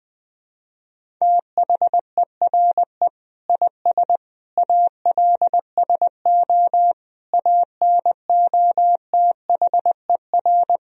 Morse code, heard as a keyed tone, about 21 wpm, 700 Hz. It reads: THERE IS ALSO ANOTHER
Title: Sentences from Top 1000 Words - Review Part 1 - 20wpm